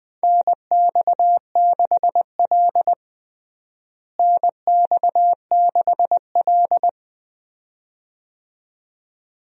Morse code, heard as a keyed tone, about 20 wpm, 700 Hz. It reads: NX6L NX6L
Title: USA Call Signs - 4 Characters - Repeated 2x - 20wpm